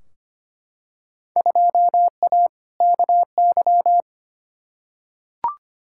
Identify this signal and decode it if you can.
Morse code, heard as a keyed tone, about 25 wpm, 700 Hz. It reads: 2A KY
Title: Field Day Exchanges - 25wpm